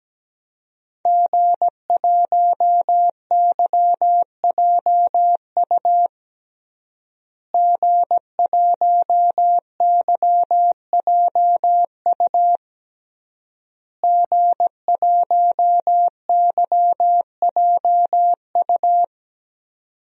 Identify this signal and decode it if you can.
Morse code, heard as a keyed tone, about 17 wpm, 700 Hz. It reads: G1YJU G1YJU G1YJU
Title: DXCC-100 Call Signs - 5 Characters - Repeated 3x - 17wpm